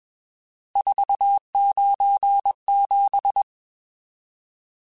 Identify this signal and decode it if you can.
Morse code, heard as a keyed tone, about 21 wpm, 800 Hz. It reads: 497
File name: RSTs / RST-mp3s / Various RST @20